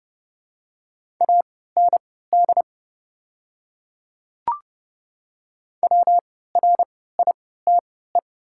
Morse code, heard as a keyed tone, about 30 wpm, spaced as for 17 wpm, 690 Hz. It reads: AND WRITE
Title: 3x Character Spacing - Top 100 Words - 30wpm